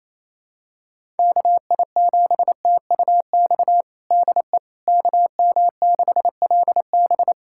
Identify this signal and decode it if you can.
Morse code, heard as a keyed tone, about 28 wpm, 700 Hz. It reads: KI7TUX DE KM6LB
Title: USA Callsign DE Callsign 28wpm